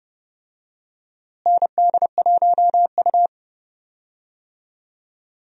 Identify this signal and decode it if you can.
Morse code, heard as a keyed tone, about 31 wpm, 700 Hz. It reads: ND1U
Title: USA Call Signs - 4 Characters - Repeated 2x - 30wpm